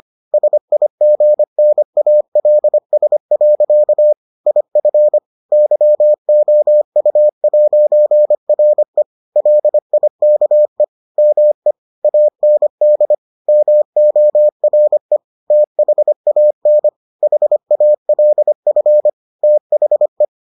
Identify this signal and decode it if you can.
Morse code, heard as a keyed tone, about 25 wpm, 600 Hz. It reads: SIGNALS. IF YOU'RE LIKE ME AND MORE THAN HALF THE